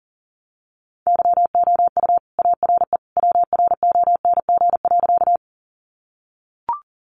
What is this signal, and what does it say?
Morse code, heard as a keyed tone, about 40 wpm, 700 Hz. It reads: YOU ARE WRONG.
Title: Sentences from Top 900 Words 40wpm